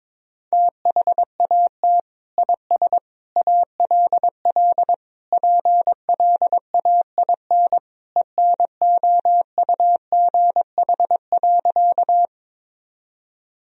Morse code, heard as a keyed tone, about 22 wpm, 700 Hz. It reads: THAT IS ALL PLAIN ENOUGH.
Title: Sentences from Top 1000 Words - Review Part 2 - 22wpm